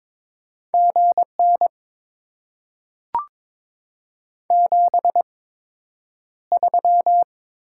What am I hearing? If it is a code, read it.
Morse code, heard as a keyed tone, about 22 wpm, spaced as for 9 wpm, 700 Hz. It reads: GN 73
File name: Top 90 QSO Elements - 8x Character Spacing - 22wpm